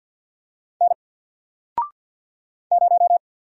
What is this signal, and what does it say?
Morse code, heard as a keyed tone, about 52 wpm, 700 Hz. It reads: N 0
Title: Single Letter-Number 50wpm